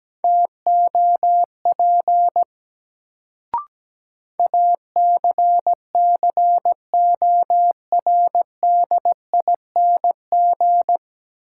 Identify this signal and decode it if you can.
Morse code, heard as a keyed tone, about 17 wpm, 700 Hz. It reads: TOP ACCORDING
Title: Sentences from Top 700 Words - New Word Drill - 17wpm